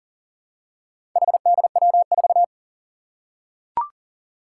Morse code, heard as a keyed tone, about 41 wpm, 700 Hz. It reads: HDW4
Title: Four Challenging Letter-Number Combinations 40wpm